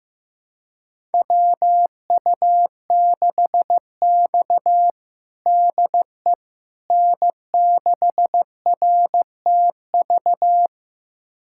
Morse code, heard as a keyed tone, about 15 wpm, 700 Hz. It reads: WU6X DE N6RTV
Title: USA Callsign DE Callsign 15wpm